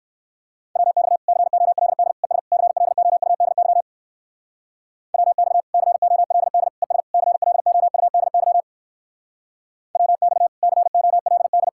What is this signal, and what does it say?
Morse code, heard as a keyed tone, about 62 wpm, 700 Hz. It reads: WX COLD ES CLOUDY WX COLD ES CLOUDY WX COLD